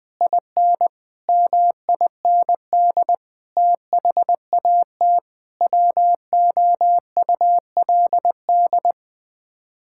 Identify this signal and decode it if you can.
Morse code, heard as a keyed tone, about 21 wpm, 700 Hz. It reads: IN MIND THAT WOULD
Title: Sets of 9 Words - 20wpm